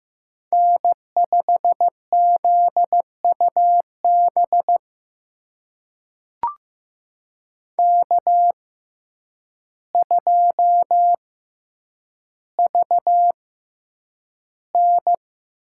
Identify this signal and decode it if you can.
Morse code, heard as a keyed tone, about 15 wpm, spaced as for 8 wpm, 700 Hz. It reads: N5ZUB K2VN
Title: USA Call Signs - 6x Character Spacing - 15wpm